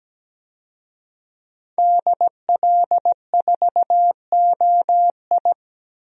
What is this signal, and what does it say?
Morse code, heard as a keyed tone, about 17 wpm, 700 Hz. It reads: DL4OI
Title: USA Call Signs - 5 Characters - Repeated 2x - 17wpm